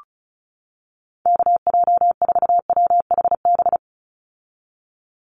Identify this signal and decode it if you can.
Morse code, heard as a keyed tone, about 34 wpm, 700 Hz. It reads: KJ4WHB
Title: USA Call Signs 35wpm